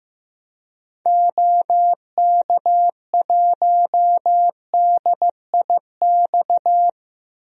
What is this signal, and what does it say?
Morse code, heard as a keyed tone, about 15 wpm, 700 Hz. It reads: OK1DIX
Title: DXCC-100 Call Signs - 6 Characters - 15wpm